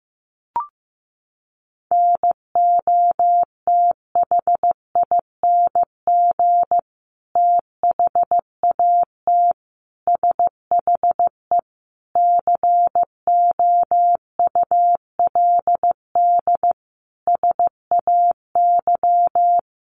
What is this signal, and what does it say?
Morse code, heard as a keyed tone, about 15 wpm, 700 Hz. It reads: NOTHING THAT SHE COULD SAY
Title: Sentences from Top 1100 Words 15wpm